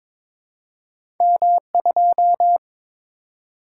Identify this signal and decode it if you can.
Morse code, heard as a keyed tone, about 23 wpm, 700 Hz. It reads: M2
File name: Two Letter-Number Combinations - 22wpm